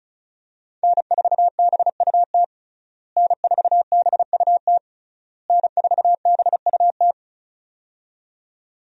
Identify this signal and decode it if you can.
Morse code, heard as a keyed tone, about 36 wpm, 700 Hz. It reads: N4BUT N4BUT N4BUT
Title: USA Call Signs - 5 Characters - Repeated 3x - 35wpm